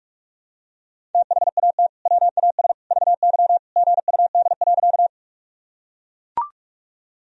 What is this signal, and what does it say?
Morse code, heard as a keyed tone, about 42 wpm, 700 Hz. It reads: THAT WAS VY GUD.
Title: Sentences from Top 200 Words and Abbreviations 45wpm